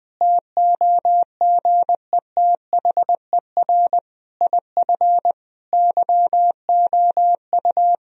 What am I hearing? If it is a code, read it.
Morse code, heard as a keyed tone, about 20 wpm, 700 Hz. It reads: TOGETHER IF YOU
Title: Sentences from Top 1000 Words - Review Part 2 - 20wpm